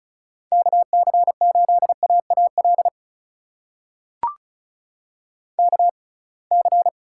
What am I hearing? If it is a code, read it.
Morse code, heard as a keyed tone, about 36 wpm, spaced as for 27 wpm, 700 Hz. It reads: KC8AAL KC
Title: USA Call Signs - 6 Characters - 6x Character Spacing - 35wpm